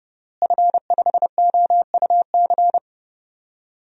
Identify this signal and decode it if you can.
Morse code, heard as a keyed tone, about 31 wpm, 700 Hz. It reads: F5OUC
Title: DXCC-100 Call Signs - 5 Characters - Repeated 2x - 30wpm